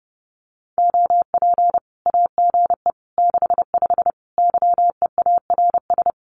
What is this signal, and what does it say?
Morse code, heard as a keyed tone, about 30 wpm, 700 Hz. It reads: OP AGE 65 YEARS
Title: Common QSO Phrases 30wpm